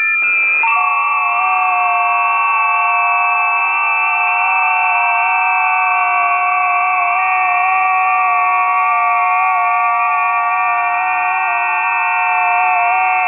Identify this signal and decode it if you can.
FT8.
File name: 20m busy / test 01